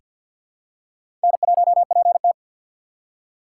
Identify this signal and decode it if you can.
Morse code, heard as a keyed tone, about 47 wpm, 700 Hz. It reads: N1PT
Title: USA Call Signs - 4x Character Spacing - 50wpm